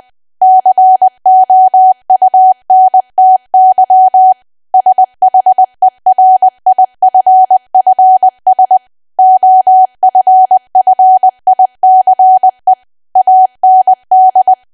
Morse code, heard as a keyed tone, about 20 wpm, 750 Hz. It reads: COUNTY SHERIFFS OFFICE AND
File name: Archive / 20 WPM / 20WPM